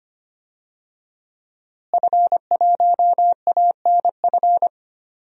Morse code, heard as a keyed tone, about 25 wpm, 700 Hz. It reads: F1ANF
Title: DXCC-100 Call Signs - 5 Characters - Repeated 3x - 25wpm